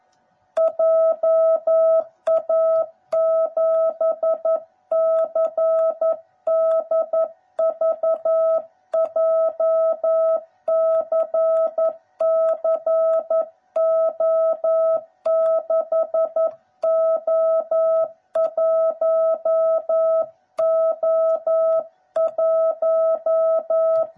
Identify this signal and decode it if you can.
Morse code, heard as a keyed tone, about 11 wpm, 660 Hz. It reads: JA7CDV JCCO6O1O1
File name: jh8cbh / 2021cw / cw09b